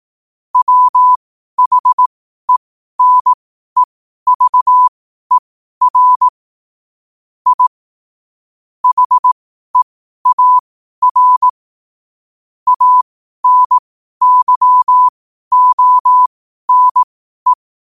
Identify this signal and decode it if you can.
Morse code, heard as a keyed tone, about 18 wpm, spaced as for 12 wpm, 1000 Hz. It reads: WHENEVER I HEAR ANYONE
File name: qotdfiles / 12wpm